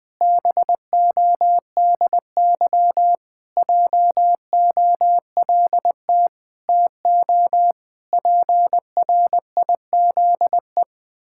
Morse code, heard as a keyed tone, about 19 wpm, 700 Hz. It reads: BODY JOLT TO PRIZE
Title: Pangram Challenge v2 - 20wpm